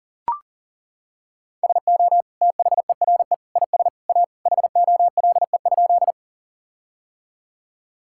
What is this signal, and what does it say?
Morse code, heard as a keyed tone, about 40 wpm, 700 Hz. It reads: SO THERE IS A HOLE?
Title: Sentences from Top 1600 Words 40wpm